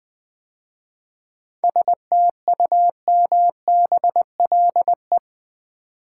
Morse code, heard as a keyed tone, about 20 wpm, 700 Hz. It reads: STUMBLE
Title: 7 Letter Words - First 3 Letters Given - 20wpm